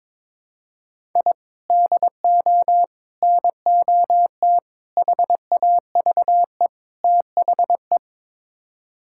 Morse code, heard as a keyed tone, about 22 wpm, 700 Hz. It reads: I DO NOT HAVE THE
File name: practice / 22wpm / Build A Phrase - 5-Word Phrases - 22wpm